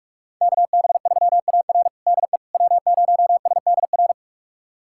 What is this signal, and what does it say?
Morse code, heard as a keyed tone, about 46 wpm, 700 Hz. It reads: KB3AR DE W0SDR
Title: USA Callsign DE Callsign - 5 Characters - 45wpm